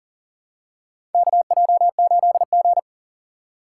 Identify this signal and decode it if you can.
Morse code, heard as a keyed tone, about 38 wpm, 700 Hz. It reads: KJ8G
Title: USA Call Signs - 4 Characters - 4x Character Spacing - 40wpm